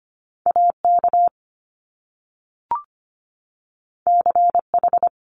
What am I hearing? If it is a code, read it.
Morse code, heard as a keyed tone, about 25 wpm, 700 Hz. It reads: AK CH
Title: Top Two Letter Combinations in Words 25wpm